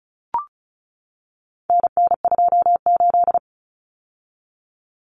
Morse code, heard as a keyed tone, about 34 wpm, 700 Hz. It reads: NN28
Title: Common Cut Numbers - 4-Digit 35wpm